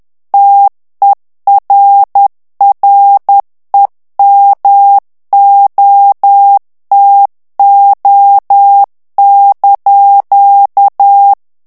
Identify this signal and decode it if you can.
Morse code, heard as a keyed tone, about 10 wpm, 800 Hz. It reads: TERREMOTO)
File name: morse mp3 / ITALIEN texte clair 1 vitesse 10 mots minute